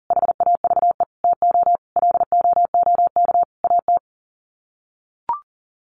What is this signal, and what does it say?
Morse code, heard as a keyed tone, about 40 wpm, 700 Hz. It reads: HAVE TO LOOK AT
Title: Sets of 5 Words Encore 40wpm